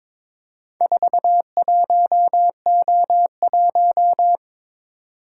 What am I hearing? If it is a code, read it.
Morse code, heard as a keyed tone, about 22 wpm, 700 Hz. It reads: 41O1
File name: practice / 22wpm / Four Letter-Number Combinations - 22wpm